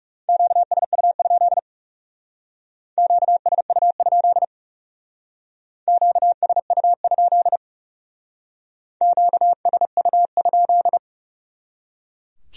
Morse code, heard as a keyed tone, about 37 wpm, 700 Hz. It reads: QSU? QSU? QSU? QSU?
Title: Speed-Racing Q-Codes - 30wpm